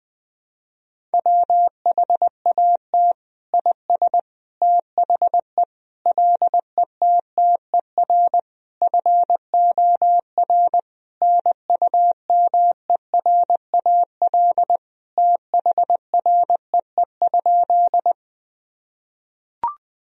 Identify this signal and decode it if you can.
Morse code, heard as a keyed tone, about 20 wpm, 700 Hz. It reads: WHAT IS THE LETTER FOR NUMERAL THREE?
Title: Sentences from Top 1000 Words - Review Part 2 - 20wpm